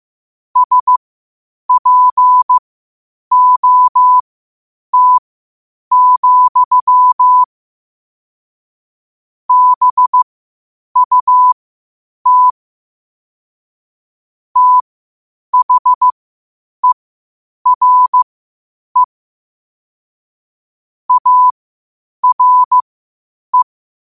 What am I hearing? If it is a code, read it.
Morse code, heard as a keyed tone, about 15 wpm, spaced as for 8 wpm, 1000 Hz. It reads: SPOT, BUT THERE ARE
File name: qotdfiles / 8wpm